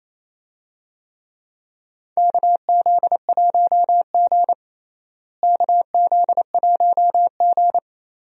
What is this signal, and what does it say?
Morse code, heard as a keyed tone, about 28 wpm, 700 Hz. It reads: KZ1G KZ1G
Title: USA Call Signs - 4 Characters - Repeated 3x - 28wpm